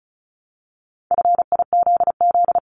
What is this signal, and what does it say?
Morse code, heard as a keyed tone, about 36 wpm, 700 Hz. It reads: FIZZ
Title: Multi-letter Phonemes Repeated 3x 35wpm